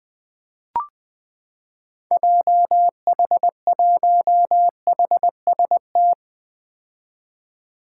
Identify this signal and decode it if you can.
Morse code, heard as a keyed tone, about 20 wpm, 700 Hz. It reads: JH1HST
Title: 2023 POTA Activator Call Signs - 20wpm